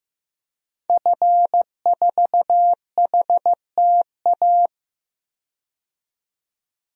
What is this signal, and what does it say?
Morse code, heard as a keyed tone, about 15 wpm, 700 Hz. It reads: F4HTA
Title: DXCC-100 Call Signs - 5 Characters - 15wpm